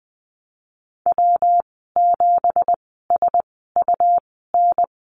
Morse code, heard as a keyed tone, about 20 wpm, spaced as for 15 wpm, 700 Hz. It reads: W7SUN
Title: 179 Lesson 29g - Slash - Call Signs 20wpm